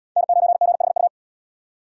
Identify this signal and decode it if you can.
Morse code, heard as a keyed tone, about 77 wpm, 700 Hz. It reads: N9GHU